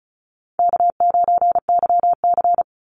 Morse code, heard as a keyed tone, about 34 wpm, 700 Hz. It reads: K9YC
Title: Call Signs Repeated 3x - 35wpm